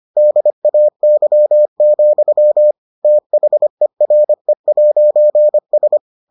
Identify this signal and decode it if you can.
Morse code, heard as a keyed tone, about 25 wpm, 600 Hz. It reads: DAY, THERE'S